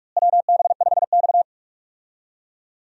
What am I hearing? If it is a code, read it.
Morse code, heard as a keyed tone, about 41 wpm, 700 Hz. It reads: WB5X
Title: USA Call Signs - 4 Characters - Repeated 3x - 45wpm